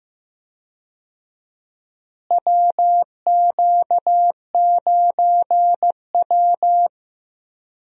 Morse code, heard as a keyed tone, about 15 wpm, 700 Hz. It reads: WQ9W
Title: USA Call Signs - 4 Characters - Repeated 2x - 15wpm